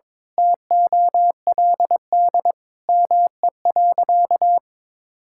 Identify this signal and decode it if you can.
Morse code, heard as a keyed tone, about 22 wpm, 700 Hz. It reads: TOLD ME.
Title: Sentences from Top 1000 Words - Review Part 6 - 22wpm